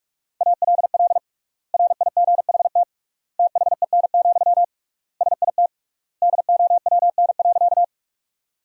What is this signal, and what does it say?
Morse code, heard as a keyed tone, about 46 wpm, 700 Hz. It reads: ALL RIGHT THEN, SIT DOWN.
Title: 3x Word-Spacing - Sentences from Top 500 Words - 45wpm